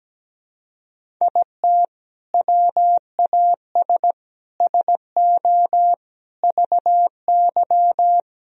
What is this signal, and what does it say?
Morse code, heard as a keyed tone, about 17 wpm, 700 Hz. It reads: IT WAS SO VY
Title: Sentences from Top 500 Words and Abbreviations 17wpm